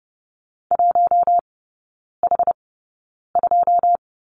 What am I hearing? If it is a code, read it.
Morse code, heard as a keyed tone, about 30 wpm, spaced as for 9 wpm, 700 Hz. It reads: 1H2
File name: 081 Lesson 14e - Number 2 - Three Characters 30wpm